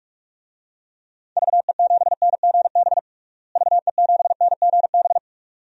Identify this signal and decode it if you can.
Morse code, heard as a keyed tone, about 46 wpm, 690 Hz. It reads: VE7NGB VE7NGB
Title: DXCC-100 Call Signs - 6 Characters - Repeated 2x - 45wpm